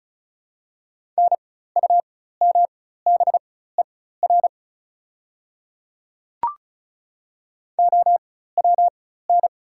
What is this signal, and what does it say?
Morse code, heard as a keyed tone, about 36 wpm, spaced as for 16 wpm, 690 Hz. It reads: NUMBER OWN
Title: 4x Character Spacing - Top 500 Words - 35wpm